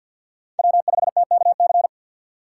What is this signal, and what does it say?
Morse code, heard as a keyed tone, about 51 wpm, 700 Hz. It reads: W5TKC
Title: USA Call Signs - 5 Characters - 2x Character Spacing - 50wpm